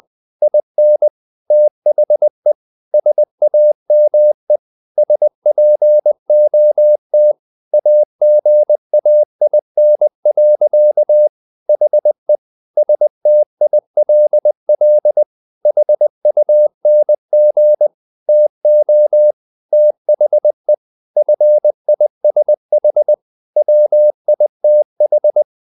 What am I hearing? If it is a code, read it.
Morse code, heard as a keyed tone, about 20 wpm, 600 Hz. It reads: IN THE SAME SPOT AGAIN. HE STILL HUNG TO THE FISH WITH
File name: the old man and the sea part 2 20wpm